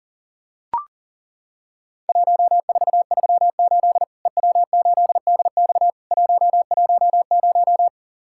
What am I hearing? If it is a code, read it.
Morse code, heard as a keyed tone, about 43 wpm, 700 Hz. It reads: 1438 EW8DX 110
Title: Worked All Europe DX Contest - QTC Exchange - 40wpm